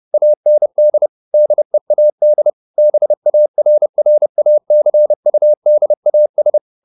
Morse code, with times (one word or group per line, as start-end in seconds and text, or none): 0.1-1.1 AND
1.3-2.5 DEAD
2.8-6.6 BARRACUDAS